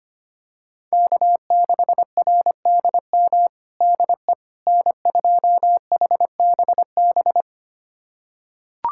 Morse code, heard as a keyed tone, about 25 wpm, 700 Hz. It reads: K6RDM DE N2HBB
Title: USA Callsign DE Callsign - 5 Characters - 25wpm